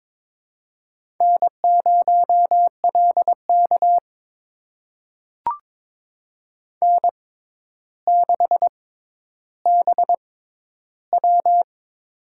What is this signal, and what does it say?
Morse code, heard as a keyed tone, about 22 wpm, spaced as for 11 wpm, 700 Hz. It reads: N0LK N6BW
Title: USA Call Signs - 4 Characters - 6x Character Spacing - 22wpm